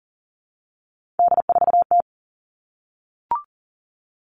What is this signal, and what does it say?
Morse code, heard as a keyed tone, about 40 wpm, 700 Hz. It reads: D4T
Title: All Cut Numbers - 3-Digit 40wpm